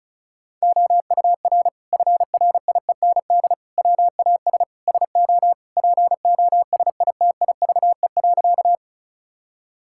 Morse code, heard as a keyed tone, about 36 wpm, 700 Hz. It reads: OUR FRIEND WAS SO POSITIVE.
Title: Sentences from Top 1000 Words - Review Part 5 - 35wpm